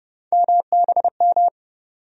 Morse code, heard as a keyed tone, about 30 wpm, 700 Hz. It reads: MBM
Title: USA Call Sign Postfixes - 3 Letters 30wpm